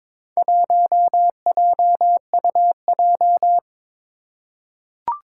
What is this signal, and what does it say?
Morse code, heard as a keyed tone, about 23 wpm, 700 Hz. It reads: 1JUJ
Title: Four Challenging Letter-Number Combinations 22wpm